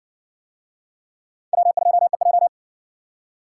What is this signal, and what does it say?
Morse code, heard as a keyed tone, about 58 wpm, 700 Hz. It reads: W2EJ